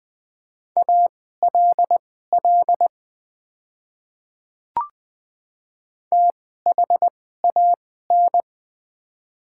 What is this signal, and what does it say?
Morse code, heard as a keyed tone, about 20 wpm, spaced as for 15 wpm, 700 Hz. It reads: ALL THAN
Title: 2x Character Spacing - Top 100 Words - 20wpm